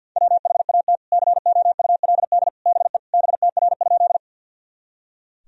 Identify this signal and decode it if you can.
Morse code, heard as a keyed tone, about 51 wpm, 700 Hz. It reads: WHAT COULD BE BTR?
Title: Sentences from Top 400 Words and Abbreviations 50wpm